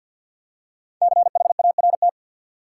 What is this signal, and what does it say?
Morse code, heard as a keyed tone, about 51 wpm, 700 Hz. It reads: CHART
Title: Five Letter Words - 50wpm